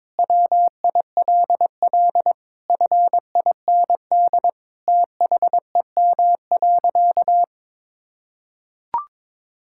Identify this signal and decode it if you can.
Morse code, heard as a keyed tone, about 22 wpm, 700 Hz. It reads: WILL FIND THEM.
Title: Sentences from Top 1000 Words - Review Part 6 - 22wpm